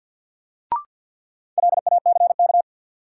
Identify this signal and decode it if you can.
Morse code, heard as a keyed tone, about 53 wpm, 700 Hz. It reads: LACK
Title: Sentences from Top 1100 Words - New Word Drill Alt - 50wpm